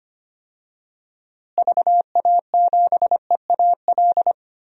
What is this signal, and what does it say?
Morse code, heard as a keyed tone, about 25 wpm, 700 Hz. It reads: VA7EAL
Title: DXCC-100 Call Signs - 6 Characters - Repeated 3x - 25wpm